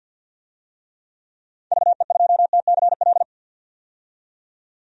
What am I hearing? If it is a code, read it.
Morse code, heard as a keyed tone, about 46 wpm, 700 Hz. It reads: VE2TCL
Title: DXCC-100 Call Signs - 6 Characters - 50wpm